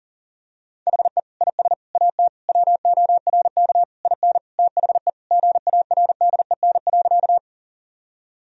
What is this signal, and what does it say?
Morse code, heard as a keyed tone, about 37 wpm, 700 Hz. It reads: HE IS AT WORK IN THE GARDEN.
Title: Sentences from Top 1000 Words - Review Part 4 - 40wpm